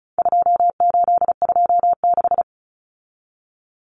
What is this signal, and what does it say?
Morse code, heard as a keyed tone, about 34 wpm, 700 Hz. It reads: 2826
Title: Numbers - 4-Digit 35wpm